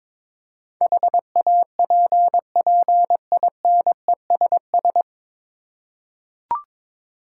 Morse code, heard as a keyed tone, about 22 wpm, 700 Hz. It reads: HAPPINESS
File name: 9 Letter Words - First 3 Letters Given - 22wpm